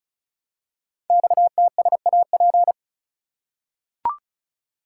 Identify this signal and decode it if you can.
Morse code, heard as a keyed tone, about 36 wpm, 700 Hz. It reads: XTSAP
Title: Code Groups - 5 Letters - 35wpm